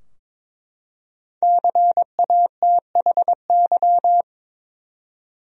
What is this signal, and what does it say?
Morse code, heard as a keyed tone, about 22 wpm, 700 Hz. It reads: CATHY
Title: USA Common Names - 22wpm